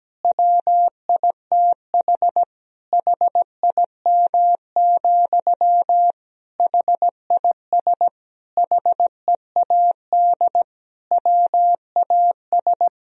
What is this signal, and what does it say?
Morse code, heard as a keyed tone, about 17 wpm, 700 Hz. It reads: WITH HIM, HIS HEAD WAS
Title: Sentences from Top 1000 Words - Review Part 2 - 17wpm